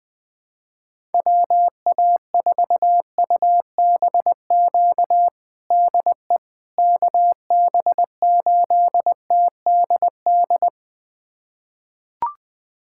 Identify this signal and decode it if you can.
Morse code, heard as a keyed tone, about 20 wpm, 700 Hz. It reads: WA4UBQ DE KB8TDD
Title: USA Callsign DE Callsign - 6 Characters - 20wpm